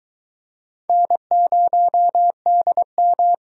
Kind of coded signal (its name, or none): Morse code